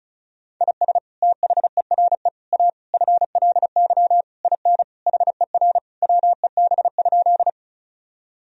Morse code, heard as a keyed tone, about 36 wpm, 700 Hz. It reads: IS THERE A FLY IN HER WEB?
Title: Sentences from Top 1000 Words - Review Part 4 - 35wpm